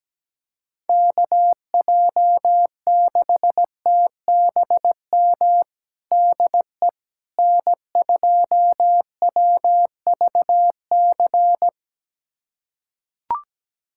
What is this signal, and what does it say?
Morse code, heard as a keyed tone, about 17 wpm, 700 Hz. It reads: KJ6TBM DE N2WVC
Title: USA Callsign DE Callsign 17wpm